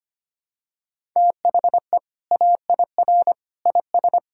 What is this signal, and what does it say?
Morse code, heard as a keyed tone, about 25 wpm, 700 Hz. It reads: THE AIR IS
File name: practice / 25wpm / Sentences with 2-Word Phrases - 25wpm